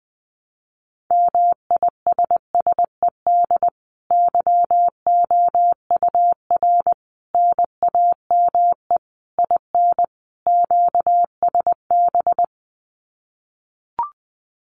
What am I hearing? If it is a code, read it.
Morse code, heard as a keyed tone, about 20 wpm, 700 Hz. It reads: MISSED YOUR NAME IN QSB
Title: Common QSO Phrases 20wpm